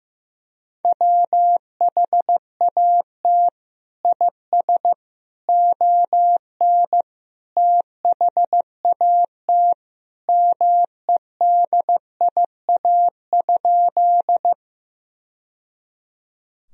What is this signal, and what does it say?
Morse code, heard as a keyed tone, about 15 wpm, 700 Hz. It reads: WHAT IS ON THAT MEDIA?
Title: Sentences from Top 1000 Words - Review Part 4 - 15wpm